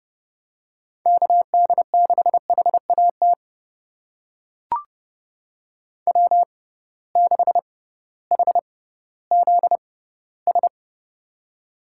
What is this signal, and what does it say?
Morse code, heard as a keyed tone, about 30 wpm, spaced as for 16 wpm, 700 Hz. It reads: KD6HAT W6HZS
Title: USA Call Signs - 6x Character Spacing - 30wpm